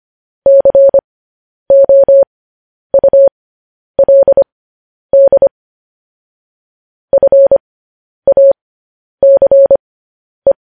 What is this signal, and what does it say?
Morse code, heard as a keyed tone, about 25 wpm, spaced as for 10 wpm, 550 Hz. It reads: COULD FACE
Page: news headlines in morse code at 10 wpm